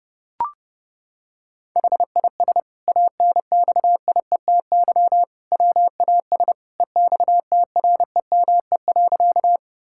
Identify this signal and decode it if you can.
Morse code, heard as a keyed tone, about 30 wpm, 700 Hz. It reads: HIS ANXIETY WAS EXTREME.
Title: Sentences from Top 1600 Words 30wpm